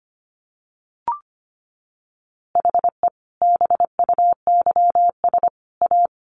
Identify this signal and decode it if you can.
Morse code, heard as a keyed tone, about 25 wpm, 700 Hz. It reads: HE BUYS A
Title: Easy Phrases 25wpm